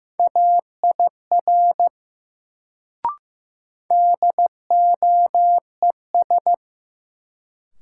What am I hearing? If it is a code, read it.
Morse code, heard as a keyed tone, about 15 wpm, 700 Hz. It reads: AIR DOES
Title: Sets of 2 Words - New Word Drill - 15wpm